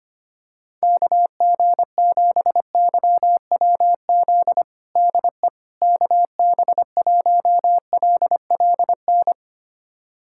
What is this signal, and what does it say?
Morse code, heard as a keyed tone, about 25 wpm, 700 Hz. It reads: KG7YWZ DE KB1LLN
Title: USA Callsign DE Callsign - 6 Characters - 25wpm